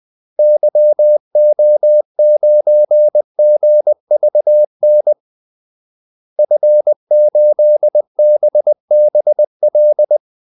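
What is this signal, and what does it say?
Morse code, heard as a keyed tone, about 20 wpm, 600 Hz. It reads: YO9GVN F8BBL
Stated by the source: Morse call-sign practice at 20 wpm.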